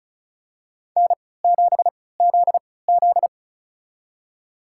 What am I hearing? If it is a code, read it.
Morse code, heard as a keyed tone, about 36 wpm, spaced as for 20 wpm, 700 Hz. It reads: N7ZZ
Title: USA Call Signs - 3x Character Spacing - 35wpm